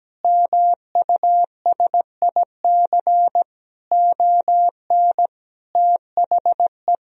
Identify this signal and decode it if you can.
Morse code, heard as a keyed tone, about 17 wpm, 700 Hz. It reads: MUSIC ON THE
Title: Sentences from Top 1000 Words - Review Part 3 - 17wpm